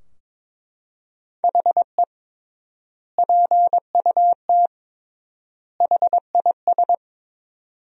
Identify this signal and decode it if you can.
Morse code, heard as a keyed tone, about 21 wpm, 700 Hz. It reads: HE PUT HIS
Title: 3x Word-Spacing - Sentences from Top 400 Words - 22wpm